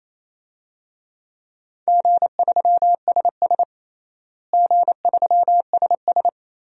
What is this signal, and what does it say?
Morse code, heard as a keyed tone, about 28 wpm, 700 Hz. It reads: G3SS G3SS
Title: DXCC-100 Call Signs - 4 Characters - Repeated 3x - 28wpm